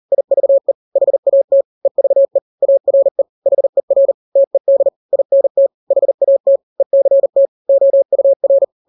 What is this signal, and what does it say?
Morse code, heard as a keyed tone, about 40 wpm, 550 Hz. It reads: IVE HAT EVE ARE HER TED INT HAT ECT OUR
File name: n-grams / 3-grams 40 35 WPM